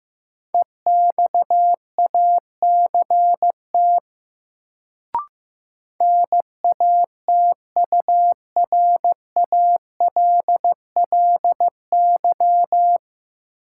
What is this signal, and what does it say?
Morse code, heard as a keyed tone, about 15 wpm, 700 Hz. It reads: EXACT NATURALLY
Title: Sentences from Top 1300 Words - New Word Drill - 15wpm